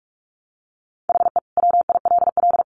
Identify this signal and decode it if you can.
Morse code, heard as a keyed tone, about 44 wpm, 700 Hz. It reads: HE WILL
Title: Additional Easy Phrases 45wpm